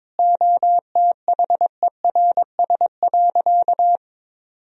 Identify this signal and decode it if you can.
Morse code, heard as a keyed tone, about 22 wpm, 700 Hz. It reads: OTHERS.
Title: Sentences from Top 1000 Words - Review Part 2 - 22wpm